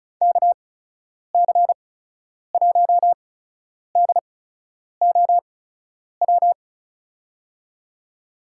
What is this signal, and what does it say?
Morse code, heard as a keyed tone, about 36 wpm, spaced as for 10 wpm, 700 Hz. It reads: KC1DOW
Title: USA Call Signs - 6 Characters - 8x Character Spacing - 35wpm